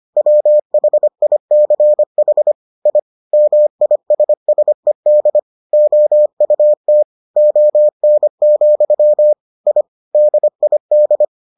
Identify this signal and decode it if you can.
Morse code, heard as a keyed tone, about 25 wpm, 600 Hz. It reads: WHICH I MISSED OUT ON, I DID